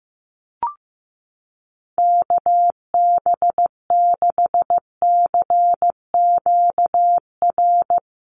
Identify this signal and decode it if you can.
Morse code, heard as a keyed tone, about 14 wpm, 700 Hz. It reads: KB6CQR
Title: USA Call Signs Repeated 3x 15wpm